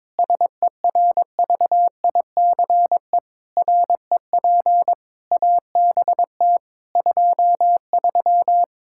Morse code, heard as a keyed tone, about 21 wpm, 700 Hz. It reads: SERVICE REP ABT 23
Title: K1USN SST QSO Practice 22wpm